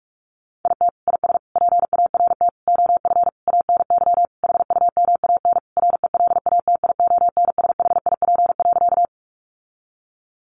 Morse code, heard as a keyed tone, about 44 wpm, 700 Hz. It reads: IT IS PART OF ANY HUMAN RELATIONSHIP.
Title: Sentences from Top 1100 Words 45wpm